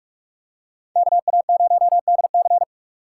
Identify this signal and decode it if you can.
Morse code, heard as a keyed tone, about 46 wpm, 700 Hz. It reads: KA0DC
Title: USA Call Signs - 5 Characters - 3x Character Spacing - 45wpm